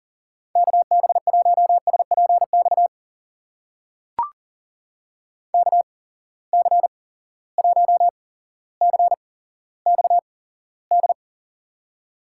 Morse code, heard as a keyed tone, about 39 wpm, spaced as for 17 wpm, 700 Hz. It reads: KB1SPX KC1CXD
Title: USA Call Signs - 6 Characters - 8x Character Spacing - 40wpm